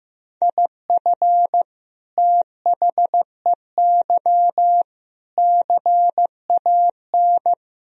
Morse code, heard as a keyed tone, about 15 wpm, 700 Hz. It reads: IF THEY CAN
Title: Sentences from Top 1000 Words - Review Part 2 - 15wpm